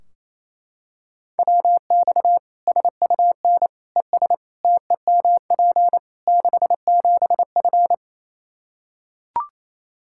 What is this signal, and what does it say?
Morse code, heard as a keyed tone, about 28 wpm, 700 Hz. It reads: WX SUN ES TEMP 67F
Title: Practice QSO - Exchange 2 - Info - 28wpm